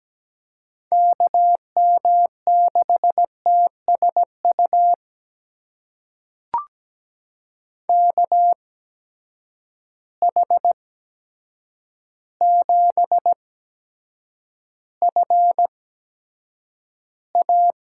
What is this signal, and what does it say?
Morse code, heard as a keyed tone, about 17 wpm, spaced as for 8 wpm, 700 Hz. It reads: KM6TSU KH7FA
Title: USA Call Signs - 8x Character Spacing - 17wpm